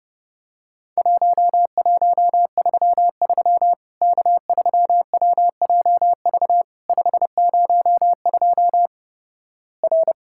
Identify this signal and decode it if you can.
Morse code, heard as a keyed tone, about 32 wpm, 700 Hz. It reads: 1133 K3WJV 502 R
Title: Worked All Europe DX Contest - QTC Exchange - 30wpm